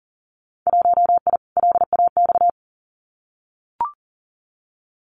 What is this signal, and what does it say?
Morse code, heard as a keyed tone, about 40 wpm, 700 Hz. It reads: 1I LAX
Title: Winter Field Day Exchanges Encore 40wpm